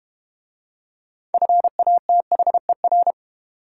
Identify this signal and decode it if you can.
Morse code, heard as a keyed tone, about 31 wpm, 690 Hz. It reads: FATHER
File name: Easy Words - Speed-Racing with Hints - 25wpm